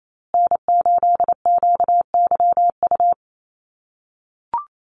Morse code, heard as a keyed tone, about 28 wpm, 700 Hz. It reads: N8QYU
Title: USA Call Signs Repeated 2x 28wpm